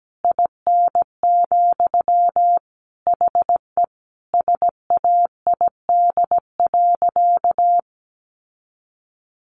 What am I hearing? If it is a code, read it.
Morse code, heard as a keyed tone, about 17 wpm, 700 Hz. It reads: IN, HE SAID.
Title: Sentences from Top 1300 Words 17wpm